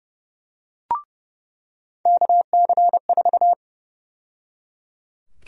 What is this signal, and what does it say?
Morse code, heard as a keyed tone, about 30 wpm, 700 Hz. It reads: KC4
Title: USA Call Sign Prefixes - 30wpm